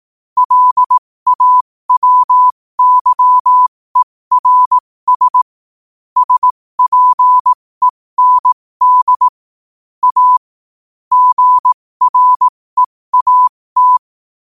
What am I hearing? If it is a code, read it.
Morse code, heard as a keyed tone, about 18 wpm, spaced as for 15 wpm, 1000 Hz. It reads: LAWYERS SPEND A GREAT
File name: qotdfiles / 15wpm